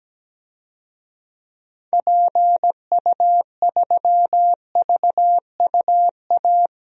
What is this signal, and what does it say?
Morse code, heard as a keyed tone, about 17 wpm, 700 Hz. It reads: PU3VUA
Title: DXCC-100 Call Signs - 6 Characters - 17wpm